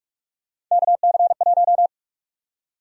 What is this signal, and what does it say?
Morse code, heard as a keyed tone, about 46 wpm, 700 Hz. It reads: KC1
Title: USA Call Sign Prefixes - 45wpm